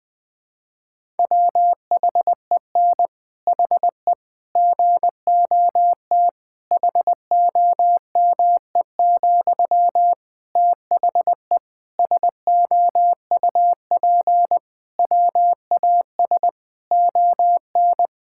Morse code, heard as a keyed tone, about 21 wpm, 700 Hz. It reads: WHEN HE GOT HOME, THE SOUP WAS ON
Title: Sentences from Top 1000 Words - Review Part 6 - 20wpm